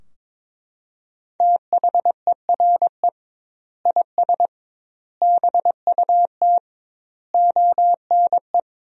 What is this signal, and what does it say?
Morse code, heard as a keyed tone, about 22 wpm, 700 Hz. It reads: THERE IS BUT ONE
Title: 2x Word-Spacing - Sentences from Top 100 Words - 22wpm